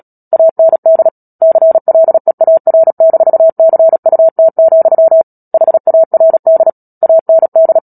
Morse code, heard as a keyed tone, about 37 wpm, 650 Hz. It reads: AND CLEAR-CUT, HARD AND
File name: Book Two / 10-Chapter 10 - Epilogue - 36 WPM